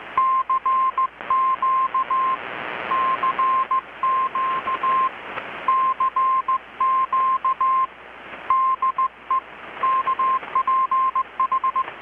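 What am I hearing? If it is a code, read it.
Morse code, heard as a keyed tone, about 20 wpm, 1050 Hz. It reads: CQ CQ CQ DE KPH
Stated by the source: Morse from KPH's Night of Nights XX maritime radiotelegraphy event, received in Chiapas, Mexico about 3,800 km away on a Kenwood R-600 with a grounded 10 m long-wire antenna.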